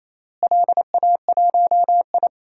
Morse code, heard as a keyed tone, about 27 wpm, 700 Hz. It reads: LA1I
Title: DXCC-100 Call Signs - 4 Characters - Repeated 3x - 28wpm